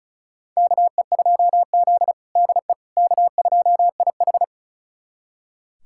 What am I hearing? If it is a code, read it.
Morse code, heard as a keyed tone, about 36 wpm, 700 Hz. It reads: KE2Z DE K2IH
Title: USA Callsign DE Callsign - 4 Characters - 35wpm